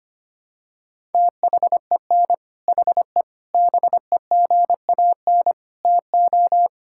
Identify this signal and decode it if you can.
Morse code, heard as a keyed tone, about 25 wpm, 700 Hz. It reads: THEN HE BEGAN TO
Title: Sentences from Top 1000 Words - Review Part 5 - 25wpm